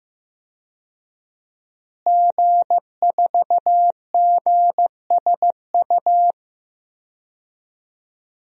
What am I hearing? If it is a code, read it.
Morse code, heard as a keyed tone, about 15 wpm, 700 Hz. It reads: G4GSU
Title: DXCC-100 Call Signs - 5 Characters - Repeated 3x - 15wpm